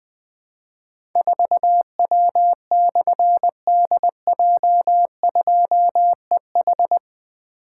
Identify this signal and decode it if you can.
Morse code, heard as a keyed tone, about 20 wpm, 700 Hz. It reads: 4W/DJ2EH
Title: Difficult Call Signs 20wpm